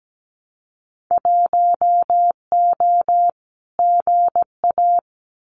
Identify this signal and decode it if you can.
Morse code, heard as a keyed tone, about 17 wpm, 700 Hz. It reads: 1O GA
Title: Winter Field Day Exchanges 17wpm